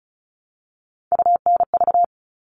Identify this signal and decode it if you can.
Morse code, heard as a keyed tone, about 34 wpm, 700 Hz. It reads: UNV
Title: All Cut Numbers - 3-Digit 35wpm